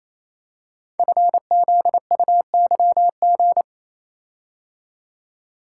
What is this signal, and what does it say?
Morse code, heard as a keyed tone, about 27 wpm, 700 Hz. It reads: FZUYG
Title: Code Groups - 5 Letter:Numbers - 28wpm